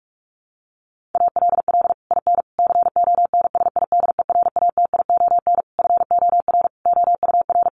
Morse code, heard as a keyed tone, about 44 wpm, 700 Hz. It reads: ALL IN CONSIDERATION FOR OUR
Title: Sentences from Top 1500 Words 45wpm